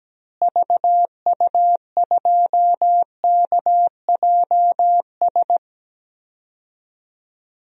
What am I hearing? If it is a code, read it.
Morse code, heard as a keyed tone, about 17 wpm, 700 Hz. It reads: VU2KJS
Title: DXCC-100 Call Signs - 6 Characters - Repeated 3x - 17wpm